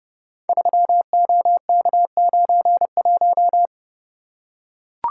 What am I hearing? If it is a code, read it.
Morse code, heard as a keyed tone, about 30 wpm, 700 Hz. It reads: 3OK91
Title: Code Groups - 5 Letter:Numbers - 30wpm